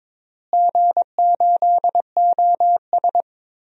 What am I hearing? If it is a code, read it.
Morse code, heard as a keyed tone, about 22 wpm, 700 Hz. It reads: G8OS
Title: DXCC-100 Call Signs - 4 Characters - Repeated 2x - 22wpm